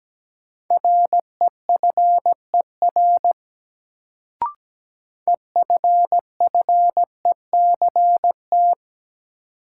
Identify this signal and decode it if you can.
Morse code, heard as a keyed tone, about 17 wpm, 700 Hz. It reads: REFER EFFECT
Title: Sentences from Top 1000 Words - New Word Drill - 17wpm